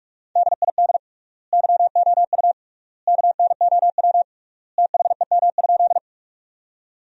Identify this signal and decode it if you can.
Morse code, heard as a keyed tone, about 41 wpm, 700 Hz. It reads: DID YOU KNOW THEM?
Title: 3x Word-Spacing - Sentences from Top 100 Words - 45wpm